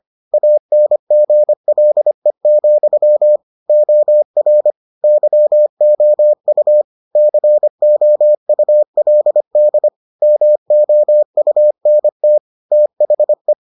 Morse code, heard as a keyed tone, about 25 wpm, 600 Hz. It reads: ANGLE, OR YOU COULD MOUNT THE